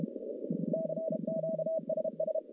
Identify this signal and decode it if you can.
Morse code, heard as a keyed tone, about 34 wpm, 640 Hz, with noise. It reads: CQ SS
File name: contests / CW-SS-k5tr- -1200